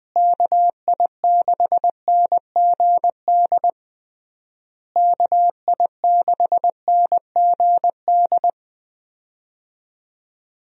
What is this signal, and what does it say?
Morse code, heard as a keyed tone, about 20 wpm, 700 Hz. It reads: KI6NGD KI6NGD
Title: USA Call Signs - 6 Characters - Repeated 3x - 20wpm